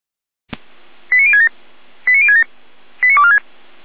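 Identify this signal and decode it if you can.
Five-tone selective call (ZVEI), heard as quick runs of five tones, tones of about 70 ms. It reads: 89067 89067 89246
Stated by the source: ZVEI five-tone call received by SDR, recorded by feuerwehr-orlamuende (fire brigade Orlamünde).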